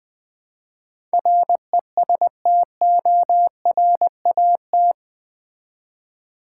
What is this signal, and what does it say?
Morse code, heard as a keyed tone, about 20 wpm, 700 Hz. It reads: RESTORAT
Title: Build A Word - 4 Letters At A Time - 11-Letter Words - 20wpm